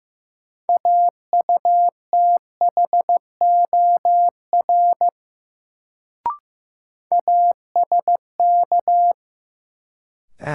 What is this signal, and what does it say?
Morse code, heard as a keyed tone, about 15 wpm, 700 Hz. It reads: AUTHOR ASK
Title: Sets of 4 Words - New Word Drill - 15wpm